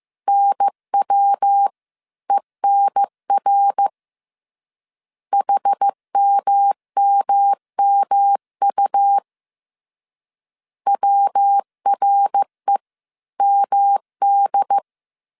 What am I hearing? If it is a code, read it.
Morse code, heard as a keyed tone, about 15 wpm, 800 Hz. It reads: NWENR HMMMU WREMD